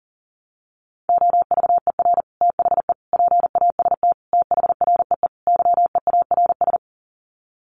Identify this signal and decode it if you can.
Morse code, heard as a keyed tone, about 40 wpm, 700 Hz. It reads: OVER THE PAST THREE YEARS
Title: Sets of 5 Words Encore2 40wpm